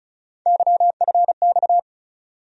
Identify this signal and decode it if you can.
Morse code, heard as a keyed tone, about 34 wpm, 700 Hz. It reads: YFX
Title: USA Call Sign Postfixes - 3 Letters 35wpm